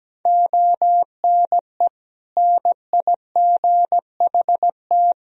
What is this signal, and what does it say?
Morse code, heard as a keyed tone, about 17 wpm, 700 Hz. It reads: ONE NIGHT
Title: Sentences from Top 300 Words and Abbreviations 17wpm